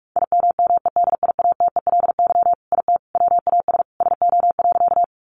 Morse code, heard as a keyed tone, about 44 wpm, 700 Hz. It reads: IMMEDIATELY IT WAS SO.
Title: Sentences from Top 1100 Words 45wpm